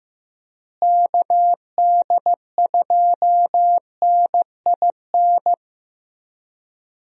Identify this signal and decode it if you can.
Morse code, heard as a keyed tone, about 15 wpm, 700 Hz. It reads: KD2NIN
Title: USA Call Signs - 6 Characters - Repeated 3x - 15wpm